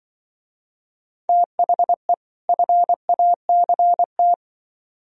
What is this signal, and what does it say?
Morse code, heard as a keyed tone, about 25 wpm, 700 Hz. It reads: THE FACT
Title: Speed-Racing - Sets of 2 Words 20wpm